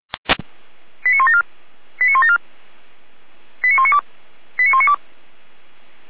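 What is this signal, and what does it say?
Five-tone selective call (ZVEI), heard as quick runs of five tones, tones of about 70 ms. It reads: 89174 89174 89192 89192
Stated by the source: Five-tone ZVEI selective call received on an SDR.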